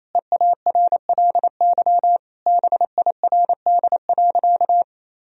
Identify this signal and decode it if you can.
Morse code, heard as a keyed tone, about 28 wpm, 700 Hz. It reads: EARLY BIRD.
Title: Sentences from Top 1000 Words - Review Part 2 - 28wpm